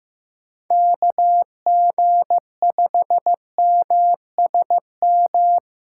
Morse code, heard as a keyed tone, about 15 wpm, 700 Hz. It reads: KG5MSM
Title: USA Call Signs - 6 Characters - 3x Character Spacing - 15wpm